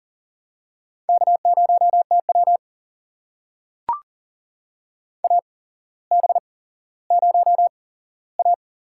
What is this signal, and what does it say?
Morse code, heard as a keyed tone, about 40 wpm, spaced as for 17 wpm, 700 Hz. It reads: K0TW AB0A
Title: USA Call Signs - 8x Character Spacing - 40wpm